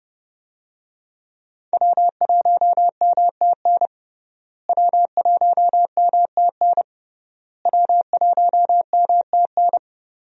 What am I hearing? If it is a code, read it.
Morse code, heard as a keyed tone, about 29 wpm, 700 Hz. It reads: W1MTN W1MTN W1MTN
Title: USA Call Signs - 5 Characters - Repeated 3x - 30wpm